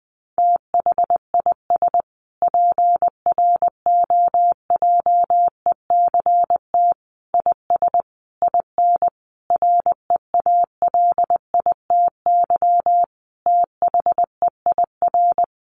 Morse code, heard as a keyed tone, about 19 wpm, 700 Hz. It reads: THIS PROJECT IS IN REALITY THEIR
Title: Sentences from Top 1400 Words 20wpm